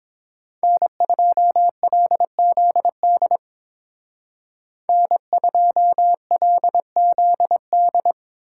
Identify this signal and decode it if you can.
Morse code, heard as a keyed tone, about 28 wpm, 700 Hz. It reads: N2LZD N2LZD